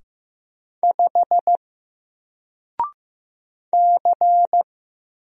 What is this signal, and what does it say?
Morse code, heard as a keyed tone, about 15 wpm, 700 Hz. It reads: 5 C
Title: Single Letter-Number Alt 15wpm